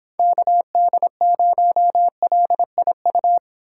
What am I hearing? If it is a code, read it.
Morse code, heard as a keyed tone, about 26 wpm, 700 Hz. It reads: KD0LIU